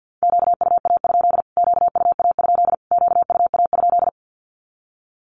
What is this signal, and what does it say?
Morse code, heard as a keyed tone, about 48 wpm, 700 Hz. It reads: QUA? QUA? QUA?
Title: Q-Codes Repeated 3x 50wpm